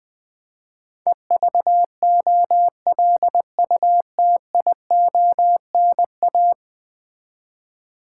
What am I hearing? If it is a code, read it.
Morse code, heard as a keyed tone, about 20 wpm, 700 Hz. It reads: EVOLUTIONA
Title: Build A Word - 5 Letters At A Time - 12-Letter Words - 20wpm